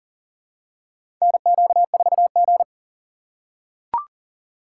Morse code, heard as a keyed tone, about 38 wpm, 700 Hz. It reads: NQ4G
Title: USA Call Signs - 2x Character Spacing - 40wpm